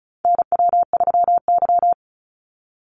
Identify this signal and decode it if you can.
Morse code, heard as a keyed tone, about 34 wpm, 700 Hz. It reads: NW3Y
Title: Call Signs - 35wpm